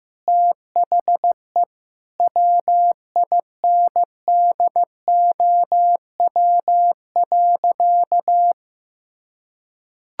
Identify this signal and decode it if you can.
Morse code, heard as a keyed tone, about 15 wpm, 700 Hz. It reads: THE WINDOW.
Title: Sentences from Top 1600 Words 15wpm